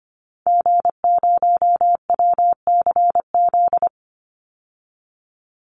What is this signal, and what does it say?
Morse code, heard as a keyed tone, about 25 wpm, 700 Hz. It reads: G0WCZ
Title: Call Signs - 25wpm